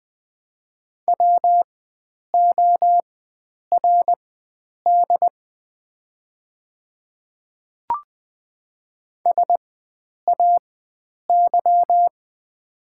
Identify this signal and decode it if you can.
Morse code, heard as a keyed tone, about 20 wpm, spaced as for 9 wpm, 700 Hz. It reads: WORD SAY
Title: 4x Character Spacing - Top 300 Words - 20wpm